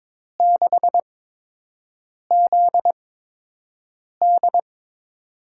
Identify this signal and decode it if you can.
Morse code, heard as a keyed tone, about 22 wpm, spaced as for 6 wpm, 700 Hz. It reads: 6ZD
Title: USA Call Signs - 5 Characters - 8x Character Spacing - 22wpm